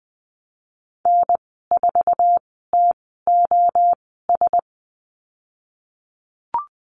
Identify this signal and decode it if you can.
Morse code, heard as a keyed tone, about 19 wpm, spaced as for 14 wpm, 700 Hz. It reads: N4TOS
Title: 088 Lesson 15g - Number 5 - Call Signs 20wpm